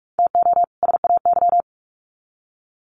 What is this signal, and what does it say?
Morse code, heard as a keyed tone, about 44 wpm, 700 Hz. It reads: TO SAY
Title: Sets of 2 Words 45wpm